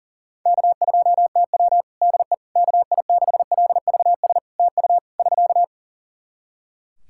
Morse code, heard as a keyed tone, about 40 wpm, 700 Hz. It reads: K2TW DE KI6LVS TU <SK>
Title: Practice QSO - Exchange 4 - Ending - 40wpm